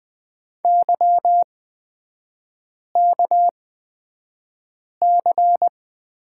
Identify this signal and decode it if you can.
Morse code, heard as a keyed tone, about 20 wpm, spaced as for 5 wpm, 700 Hz. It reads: YKC
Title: 236 Lesson 38e - Letter q - Three Characters 20wpm